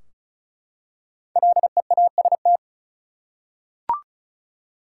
Morse code, heard as a keyed tone, about 34 wpm, 700 Hz. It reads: LEAST
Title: Five Letter Words - 35wpm